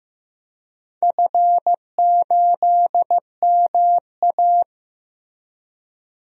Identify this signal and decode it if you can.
Morse code, heard as a keyed tone, about 15 wpm, 700 Hz. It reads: F8MA
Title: DXCC-100 Call Signs - 4 Characters - Repeated 2x - 15wpm